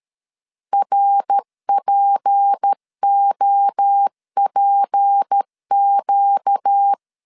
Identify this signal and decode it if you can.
Morse code, heard as a keyed tone, about 13 wpm, 800 Hz. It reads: RPOPQ
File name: LEZIONI / 6 PKX0 / 6 P 12wpm 2033